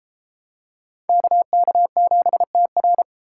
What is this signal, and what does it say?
Morse code, heard as a keyed tone, about 31 wpm, 700 Hz. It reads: KK7TR